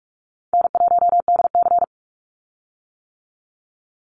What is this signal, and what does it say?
Morse code, heard as a keyed tone, about 46 wpm, 700 Hz. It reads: N1DC
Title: Easy Call Signs - 45wpm